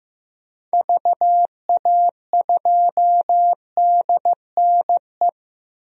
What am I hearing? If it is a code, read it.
Morse code, heard as a keyed tone, about 16 wpm, 700 Hz. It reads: VA2DNE
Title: DXCC-100 Call Signs - 6 Characters - Repeated 2x - 15wpm